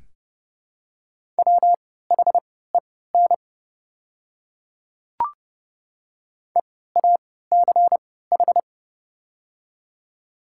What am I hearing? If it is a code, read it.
Morse code, heard as a keyed tone, about 30 wpm, spaced as for 17 wpm, 700 Hz. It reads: WHEN EACH
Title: 3x Character Spacing - Top 200 Words - 30wpm